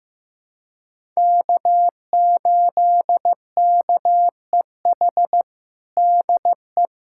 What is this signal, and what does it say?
Morse code, heard as a keyed tone, about 15 wpm, 700 Hz. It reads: K8KEH DE
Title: USA Callsign DE Callsign - 5 Characters - 15wpm